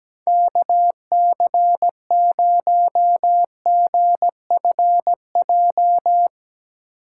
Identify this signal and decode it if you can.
Morse code, heard as a keyed tone, about 17 wpm, 700 Hz. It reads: KC0GFJ
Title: USA Call Signs - 6 Characters - 8x Character Spacing - 17wpm